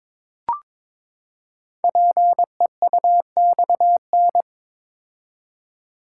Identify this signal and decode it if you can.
Morse code, heard as a keyed tone, about 22 wpm, 700 Hz. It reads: PEUXN
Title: Code Groups - 5 Letters - 22wpm